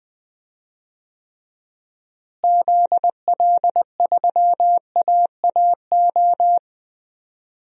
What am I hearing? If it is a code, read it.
Morse code, heard as a keyed tone, about 20 wpm, 700 Hz. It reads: ZL3AAO
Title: DXCC-100 Call Signs - 6 Characters - Repeated 2x - 20wpm